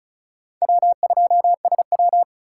Morse code, heard as a keyed tone, about 36 wpm, 700 Hz. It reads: W2SW
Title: USA Call Signs - 4 Characters - 2x Character Spacing - 35wpm